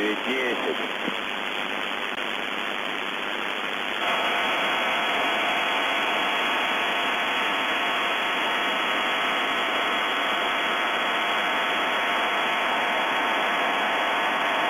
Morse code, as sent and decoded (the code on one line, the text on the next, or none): none